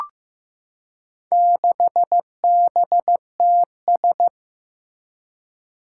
Morse code, heard as a keyed tone, about 15 wpm, 700 Hz. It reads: 6BTS
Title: Four Challenging Letter-Number Combinations 15wpm